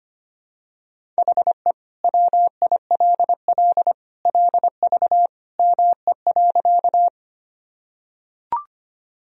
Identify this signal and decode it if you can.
Morse code, heard as a keyed tone, about 25 wpm, 700 Hz. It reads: HE WILL LV ME.
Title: Sentences from Top 400 Words and Abbreviations 25wpm